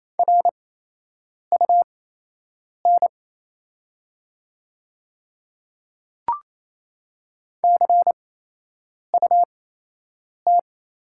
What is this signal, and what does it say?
Morse code, heard as a keyed tone, about 28 wpm, spaced as for 8 wpm, 700 Hz. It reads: RUN CUT
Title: 8x Character Spacing - Top 500 Words - 28wpm